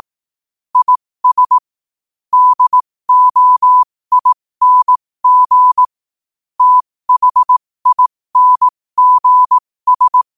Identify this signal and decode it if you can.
Morse code, heard as a keyed tone, about 18 wpm, spaced as for 15 wpm, 1000 Hz. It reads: IS DOING THINGS